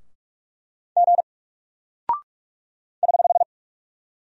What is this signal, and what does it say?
Morse code, heard as a keyed tone, about 43 wpm, 700 Hz. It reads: G <HH>
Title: Single Character (Letter:Number:Punctuation:Prosign) Alt 45wpm